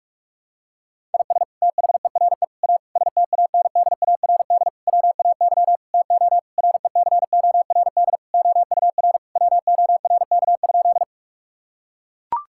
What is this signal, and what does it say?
Morse code, heard as a keyed tone, about 42 wpm, 700 Hz. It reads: IS THERE A STANDARD WAY TO RECORD OUR WORK?
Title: Sentences from Top 1000 Words - Review Part 6 - 45wpm